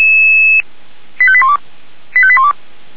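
Five-tone selective call (ZVEI), heard as quick runs of five tones, tones of about 70 ms. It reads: 85812 85812
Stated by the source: ZVEI five-tone selective call received on an SDR.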